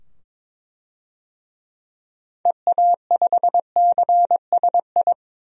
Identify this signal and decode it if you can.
Morse code, heard as a keyed tone, about 22 wpm, 700 Hz. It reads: EA5CSI
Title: DXCC-100 Call Signs - 6 Characters - 22wpm